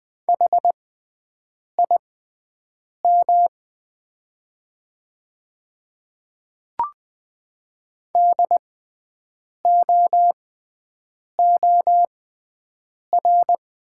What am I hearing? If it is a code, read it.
Morse code, heard as a keyed tone, about 20 wpm, spaced as for 7 wpm, 690 Hz. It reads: HIM DOOR
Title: 6x Character Spacing - Top 300 Words - 20wpm